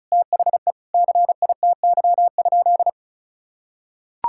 Morse code, heard as a keyed tone, about 34 wpm, 700 Hz. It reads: THE CITY?
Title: Sentences from Top 1000 Words - Review Part 5 - 35wpm